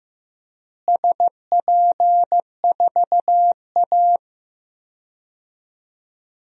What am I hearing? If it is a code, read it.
Morse code, heard as a keyed tone, about 15 wpm, 700 Hz. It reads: SP4A
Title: DXCC-100 Call Signs - 4 Characters - Repeated 3x - 15wpm